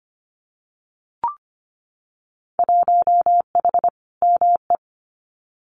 Morse code, heard as a keyed tone, about 25 wpm, 700 Hz. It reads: 1H ME
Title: Winter Field Day Exchanges 25wpm